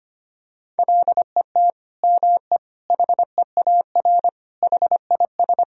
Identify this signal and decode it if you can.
Morse code, heard as a keyed tone, about 25 wpm, 700 Hz. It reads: LET ME HEAR HIS
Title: Sentences from Top 1000 Words - Review Part 3 - 25wpm